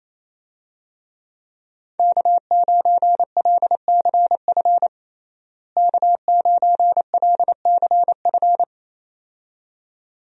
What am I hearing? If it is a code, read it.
Morse code, heard as a keyed tone, about 28 wpm, 700 Hz. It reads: K9LCF K9LCF
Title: USA Call Signs - 5 Characters - Repeated 2x - 28wpm